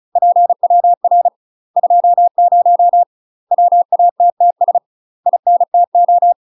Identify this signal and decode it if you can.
Morse code, heard as a keyed tone, about 36 wpm, 700 Hz. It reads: PWR 20 WATTS INTO
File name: UK MASTER QSOs TRAINING / 35WPM / UK MASTER QSOs 0009